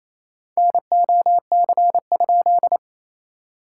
Morse code, heard as a keyed tone, about 28 wpm, 700 Hz. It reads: NOC?
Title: SOTA QSO Practice 28wpm